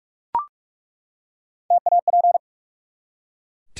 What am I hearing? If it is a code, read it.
Morse code, heard as a keyed tone, about 43 wpm, 700 Hz. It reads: TAP
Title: Three Letter Words - 45wpm